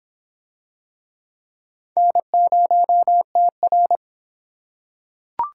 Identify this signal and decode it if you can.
Morse code, heard as a keyed tone, about 25 wpm, 700 Hz. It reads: N0TR